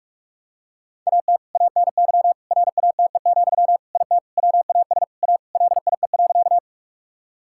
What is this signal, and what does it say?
Morse code, heard as a keyed tone, about 41 wpm, 700 Hz. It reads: AT ANY RATE, IT WAS A LIE.
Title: Sentences from Top 1600 Words 45wpm